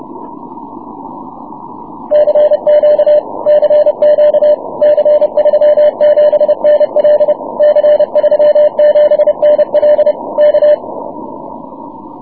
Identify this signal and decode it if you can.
Morse code, heard as a keyed tone, about 30 wpm, 600 Hz. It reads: CQ CQ C37NL C37NL K
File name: wp / c37nl 6m cw